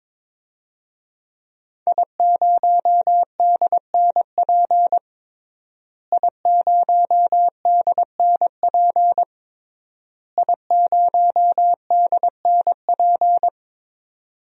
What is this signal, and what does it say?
Morse code, heard as a keyed tone, about 22 wpm, 700 Hz. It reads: I0DNP I0DNP I0DNP
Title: DXCC-100 Call Signs - 5 Characters - Repeated 3x - 22wpm